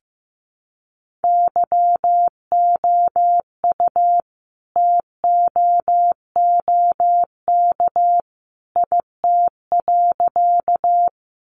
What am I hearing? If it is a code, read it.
Morse code, heard as a keyed tone, about 15 wpm, 700 Hz. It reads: YOU TOOK IT.
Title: Sentences from Top 300 Words 15wpm